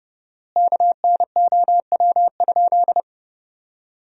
Morse code, heard as a keyed tone, about 30 wpm, 700 Hz. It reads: KNOW?
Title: 4x Word-Spacing - Sentences from Top 100 Words - 30wpm